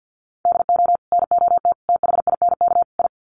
Morse code, heard as a keyed tone, about 50 wpm, 700 Hz. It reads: DO NOT THINK I
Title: Sets of 4 Words 50wpm